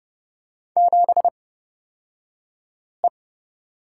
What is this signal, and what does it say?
Morse code, heard as a keyed tone, about 30 wpm, spaced as for 5 wpm, 700 Hz. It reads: 7E
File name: Lesson 38d - Letter q - Two Characters 30wpm